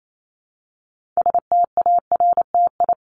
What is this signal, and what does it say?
Morse code, heard as a keyed tone, about 28 wpm, 700 Hz. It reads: STARTI
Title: ING Suffix - Top 100 Words with Base Words 28wpm